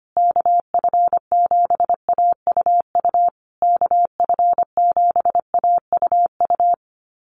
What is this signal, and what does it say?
Morse code, heard as a keyed tone, about 25 wpm, 700 Hz. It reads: KF7AUU KF7AUU
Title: USA Call Signs Repeated 3x 25wpm